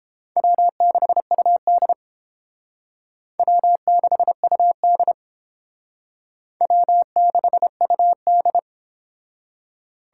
Morse code, heard as a keyed tone, about 30 wpm, 700 Hz. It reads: W6UD W6UD W6UD